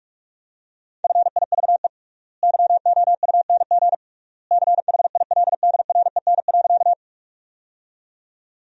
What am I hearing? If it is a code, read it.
Morse code, heard as a keyed tone, about 46 wpm, 700 Hz. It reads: FIVE YOUNG CHILDREN.
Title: 3x Word-Spacing - Sentences from Top 400 Words - 45wpm